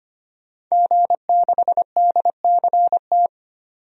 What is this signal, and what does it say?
Morse code, heard as a keyed tone, about 25 wpm, 700 Hz. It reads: G6DCT
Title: DXCC-100 Call Signs - 5 Characters - 25wpm